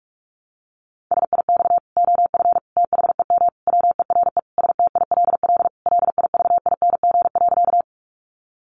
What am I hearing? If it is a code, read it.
Morse code, heard as a keyed tone, about 44 wpm, 700 Hz. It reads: SIX OF THEM WERE STILL LIVING.
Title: Sentences from Top 1100 Words 45wpm